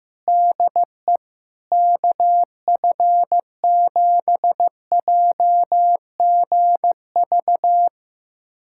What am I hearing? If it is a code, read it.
Morse code, heard as a keyed tone, about 15 wpm, 700 Hz. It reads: DE KF7JGV
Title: USA Callsign DE Callsign 15wpm